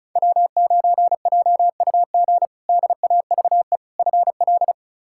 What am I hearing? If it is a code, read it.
Morse code, heard as a keyed tone, about 36 wpm, 700 Hz. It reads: W9JUG DAVE FL
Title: K1USN SST QSO Practice - 35wpm